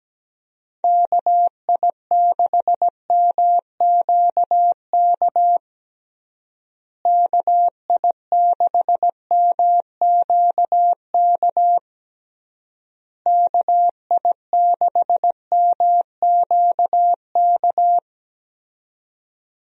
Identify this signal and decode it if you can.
Morse code, heard as a keyed tone, about 17 wpm, 700 Hz. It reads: KI6MQK KI6MQK KI6MQK
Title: USA Call Signs - 6 Characters - Repeated 3x - 17wpm